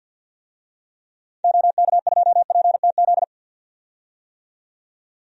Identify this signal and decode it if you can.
Morse code, heard as a keyed tone, about 51 wpm, 700 Hz. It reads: OK2PTZ
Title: DXCC-100 Call Signs - 6 Characters - 50wpm